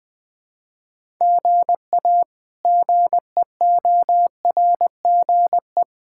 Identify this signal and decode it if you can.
Morse code, heard as a keyed tone, about 20 wpm, 700 Hz. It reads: GA GEORGE
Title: K1USN SST QSO Practice - 20wpm